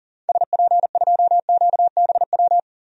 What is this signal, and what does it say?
Morse code, heard as a keyed tone, about 40 wpm, 700 Hz. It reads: SP2QBW
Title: DXCC-100 Call Signs - 6 Characters - 40wpm